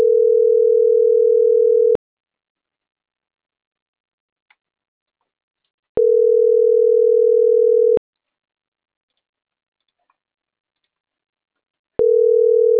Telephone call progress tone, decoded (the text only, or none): ringback tone